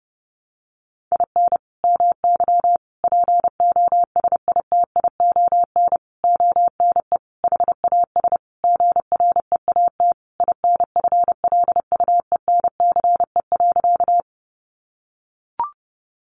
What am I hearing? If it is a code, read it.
Morse code, heard as a keyed tone, about 30 wpm, 700 Hz. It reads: IN MY POSITION ONE HAS GREAT INFLUENCE.
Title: Sentences from Top 1100 Words 30wpm